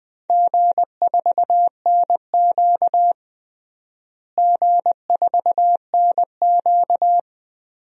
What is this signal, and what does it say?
Morse code, heard as a keyed tone, about 20 wpm, 700 Hz. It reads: G4NQ G4NQ
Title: DXCC-100 Call Signs - 4 Characters - Repeated 2x - 20wpm